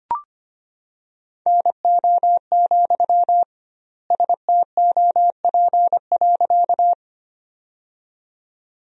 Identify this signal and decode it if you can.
Morse code, heard as a keyed tone, about 25 wpm, 700 Hz. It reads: NO, STOP.
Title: 2x Word-Spacing - Sentences from Top 300 Words - 25wpm